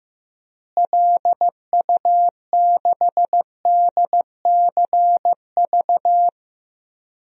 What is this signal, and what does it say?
Morse code, heard as a keyed tone, about 15 wpm, 700 Hz. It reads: LU6DCV
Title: DXCC-100 Call Signs - 6 Characters - Repeated 2x - 15wpm